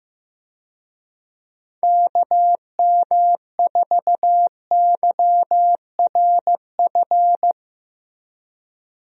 Morse code, heard as a keyed tone, about 15 wpm, 700 Hz. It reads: KM4YRF
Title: USA Call Signs - 6 Characters - Repeated 3x - 15wpm